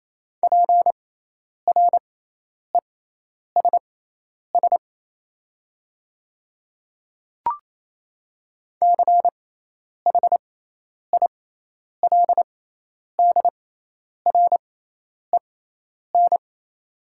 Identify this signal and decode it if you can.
Morse code, heard as a keyed tone, about 28 wpm, spaced as for 10 wpm, 700 Hz. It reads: PRESS CHILDREN
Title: 6x Character Spacing - Top 300 Words - 28wpm